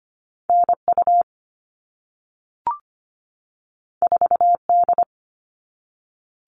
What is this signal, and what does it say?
Morse code, heard as a keyed tone, about 25 wpm, 700 Hz. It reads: NU 4D
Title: All Cut Numbers - 2-Digit 25wpm